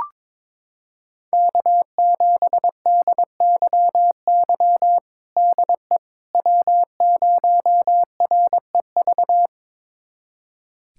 Morse code, heard as a keyed tone, about 22 wpm, 700 Hz. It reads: K7DYY DE W0REV
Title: USA Callsign DE Callsign - 5 Characters - 22wpm